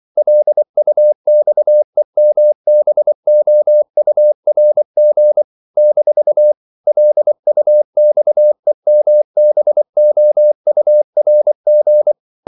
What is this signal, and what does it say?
Morse code, heard as a keyed tone, about 24 wpm, 600 Hz. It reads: LUXEMBOURG - LUXEMBOURG
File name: CW / Practice / NationalCapitals24